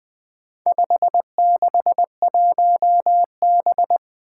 Morse code, heard as a keyed tone, about 20 wpm, 700 Hz. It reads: 561B
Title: Four Challenging Letter-Number Combinations 20wpm